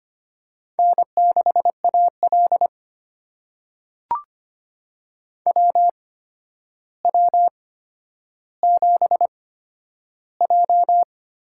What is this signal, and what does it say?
Morse code, heard as a keyed tone, about 25 wpm, spaced as for 11 wpm, 700 Hz. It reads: N6AL WW7J
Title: USA Call Signs - 4 Characters - 8x Character Spacing - 25wpm